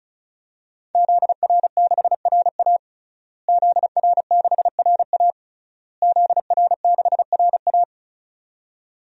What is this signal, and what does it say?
Morse code, heard as a keyed tone, about 34 wpm, 700 Hz. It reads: ZR6RA ZR6RA ZR6RA
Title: DXCC-100 Call Signs - 5 Characters - Repeated 3x - 35wpm